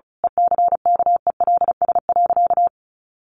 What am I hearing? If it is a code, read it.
Morse code, heard as a keyed tone, about 34 wpm, 700 Hz. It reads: ECKELS.
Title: A Sound Of Thunder 35wpm